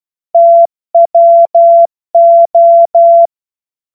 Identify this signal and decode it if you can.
Morse code, heard as a keyed tone, about 12 wpm, 680 Hz. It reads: TWO
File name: Morsekurs / Witze